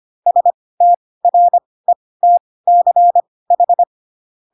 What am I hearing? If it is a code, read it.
Morse code, heard as a keyed tone, about 25 wpm, spaced as for 18 wpm, 700 Hz. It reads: STRETCH
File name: Pinocchio ch3 12 25 ik1hge